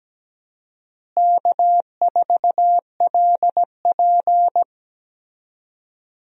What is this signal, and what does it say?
Morse code, heard as a keyed tone, about 17 wpm, 700 Hz. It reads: K4LP
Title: USA Call Signs - 4 Characters - Repeated 2x - 17wpm